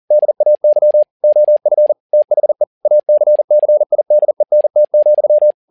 Morse code, heard as a keyed tone, about 40 wpm, 600 Hz. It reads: DAY OF THE ACCIDENT,
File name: learning center / tower trouble / tower touble 40wpm